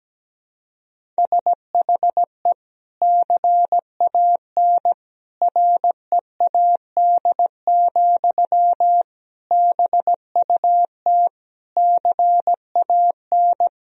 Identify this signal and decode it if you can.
Morse code, heard as a keyed tone, about 17 wpm, 700 Hz. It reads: SHE CAN READ, BUT CAN
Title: Sentences from Top 1000 Words - Review Part 1 - 17wpm